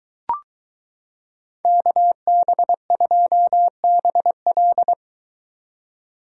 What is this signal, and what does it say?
Morse code, heard as a keyed tone, about 23 wpm, 700 Hz. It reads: KB2BL